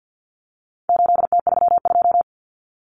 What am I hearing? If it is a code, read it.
Morse code, heard as a keyed tone, about 50 wpm, 700 Hz. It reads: 8T32
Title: Common Cut Numbers - 4-Digit 50wpm